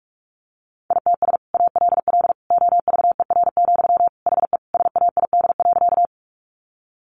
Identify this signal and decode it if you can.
Morse code, heard as a keyed tone, about 44 wpm, 700 Hz. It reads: ITS ALL OVER, HE SAID.
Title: Sentences from Top 600 Words 45wpm